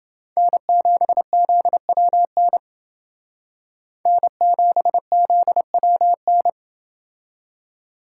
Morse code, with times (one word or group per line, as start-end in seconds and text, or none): 0.4-2.6 N7ZWN
4.1-6.5 N7ZWN